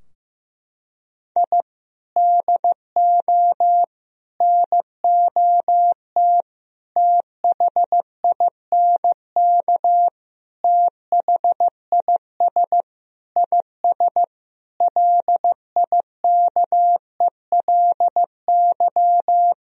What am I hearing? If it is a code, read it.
Morse code, heard as a keyed tone, about 16 wpm, 700 Hz. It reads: I DO NOT THINK THIS IS LIKELY
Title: Sentences from Top 1000 Words - Review Part 4 - 15wpm